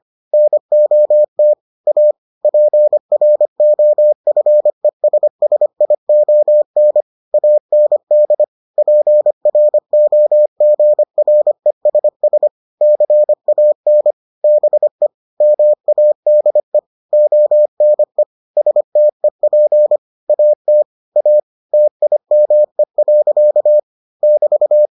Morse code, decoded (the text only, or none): NOT A PROFESSION AND PROGRESS CAN BE MADE ONE STEP AT A TIME. <BT>